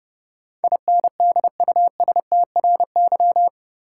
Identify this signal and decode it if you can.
Morse code, heard as a keyed tone, about 30 wpm, 700 Hz. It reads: INDUSTRY
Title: Sentences from Top 1000 Words - Review Part 4 - New Word Drill - 30wpm